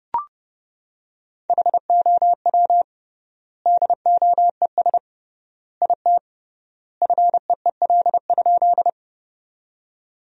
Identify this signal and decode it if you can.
Morse code, heard as a keyed tone, about 32 wpm, 700 Hz. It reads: HOW DOES IT FEEL?
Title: 3x Word-Spacing - Sentences from Top 400 Words - 30wpm